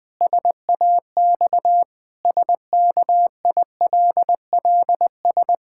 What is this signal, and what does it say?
Morse code, heard as a keyed tone, about 20 wpm, 700 Hz. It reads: SAX SKILLS
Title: Pangram Challenge v2 - 20wpm